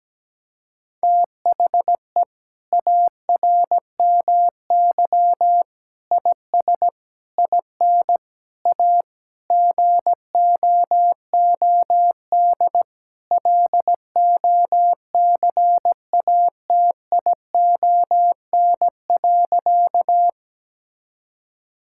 Morse code, heard as a keyed tone, about 17 wpm, 700 Hz. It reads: THE ARMY IS IN A GOOD LOCATION.
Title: Sentences from Top 1000 Words - Review Part 6 - 17wpm